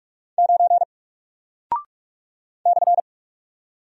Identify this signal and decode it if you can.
Morse code, heard as a keyed tone, about 48 wpm, 700 Hz. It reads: 9 /
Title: Single Character (Letter:Number:Punctuation:Prosign) Alt 45wpm